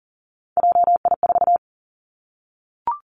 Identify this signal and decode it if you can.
Morse code, heard as a keyed tone, about 40 wpm, 690 Hz. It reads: JI4
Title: Three Challenging Letter-Number Combinations 40wpm